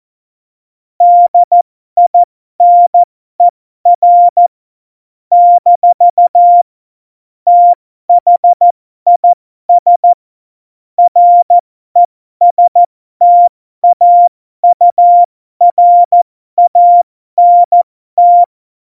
Morse code, heard as a keyed tone, about 14 wpm, spaced as for 12 wpm, 700 Hz. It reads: DINER - THIS RESTAURANT
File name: Morsekurs / Witze